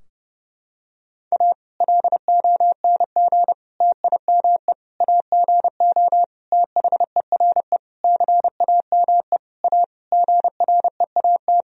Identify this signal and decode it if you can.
Morse code, heard as a keyed tone, about 29 wpm, 700 Hz. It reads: A LONG TIME AGO THERE CAME A GREAT
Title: Sentences from Top 1000 Words - Review Part 3 - 30wpm